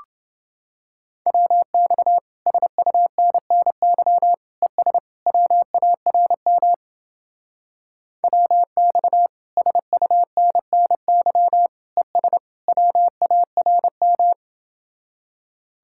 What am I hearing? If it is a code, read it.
Morse code, heard as a keyed tone, about 30 wpm, 700 Hz. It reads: WX SUNNY ES WARM WX SUNNY ES WARM